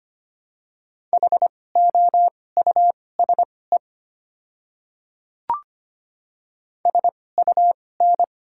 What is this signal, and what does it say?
Morse code, heard as a keyed tone, about 25 wpm, spaced as for 18 wpm, 700 Hz. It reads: HOUSE SUN
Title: 2x Character Spacing - Top 300 Words - 25wpm